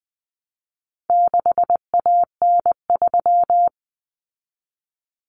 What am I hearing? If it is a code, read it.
Morse code, heard as a keyed tone, about 20 wpm, 700 Hz. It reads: 6AN3
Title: Common Cut Numbers - 4-Digit 20wpm